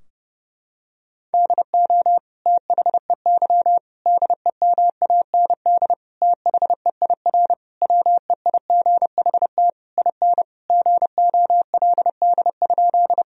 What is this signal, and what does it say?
Morse code, heard as a keyed tone, about 30 wpm, 700 Hz. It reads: DO THEY DEMAND THEIR WEIGHT IN GOLD?
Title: Sentences from Top 1000 Words - Review Part 3 - 30wpm